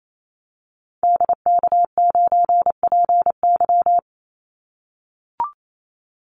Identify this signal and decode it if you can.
Morse code, heard as a keyed tone, about 28 wpm, 700 Hz. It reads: DK9PY
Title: Call Signs - 28wpm